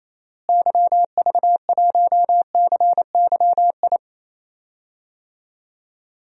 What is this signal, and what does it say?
Morse code, heard as a keyed tone, about 28 wpm, 700 Hz. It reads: YV1CYI
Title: DXCC-100 Call Signs - 6 Characters - Repeated 2x - 28wpm